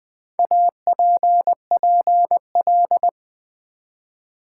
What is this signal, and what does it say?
Morse code, heard as a keyed tone, about 20 wpm, 690 Hz. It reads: APPL
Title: Build A Word - 1 Letter At A Time - 5-Letter Words - 20wpm